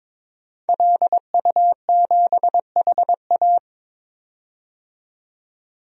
Morse code, heard as a keyed tone, about 22 wpm, 700 Hz. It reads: LU7HA
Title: DXCC-100 Call Signs - 5 Characters - 22wpm